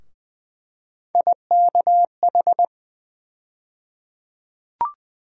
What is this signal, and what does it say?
Morse code, heard as a keyed tone, about 20 wpm, 700 Hz. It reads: IKH
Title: USA Call Sign Postfixes - 3 Letters 20wpm